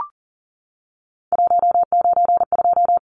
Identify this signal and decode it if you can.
Morse code, heard as a keyed tone, about 39 wpm, 700 Hz. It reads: 192
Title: Numbers - 3-Digit 40wpm